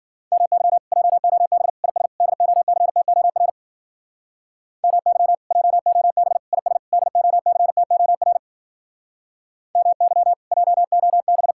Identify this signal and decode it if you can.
Morse code, heard as a keyed tone, about 52 wpm, 700 Hz. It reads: MY JOB IS DOCTOR MY JOB IS DOCTOR MY JOB